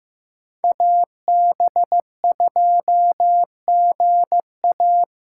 Morse code, heard as a keyed tone, about 15 wpm, 700 Hz. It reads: AB2GA
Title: USA Call Signs - 5 Characters - 3x Character Spacing - 15wpm